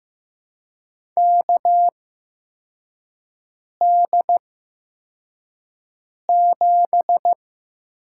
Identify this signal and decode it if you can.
Morse code, heard as a keyed tone, about 15 wpm, spaced as for 4 wpm, 700 Hz. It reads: KD7
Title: USA Call Signs - 8x Character Spacing - 15wpm